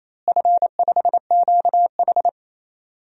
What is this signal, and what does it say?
Morse code, heard as a keyed tone, about 28 wpm, 700 Hz. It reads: F5QH
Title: DXCC-100 Call Signs - 4 Characters - Repeated 3x - 28wpm